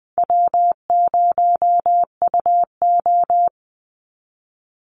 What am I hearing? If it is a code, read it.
Morse code, heard as a keyed tone, about 20 wpm, 700 Hz. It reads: W0UO
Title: Call Signs - 20wpm